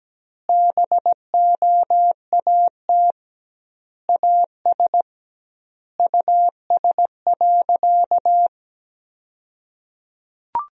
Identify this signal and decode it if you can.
Morse code, heard as a keyed tone, about 17 wpm, 700 Hz. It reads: BOAT AS US.
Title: 2x Word-Spacing - Sentences from Top 500 Words - 17wpm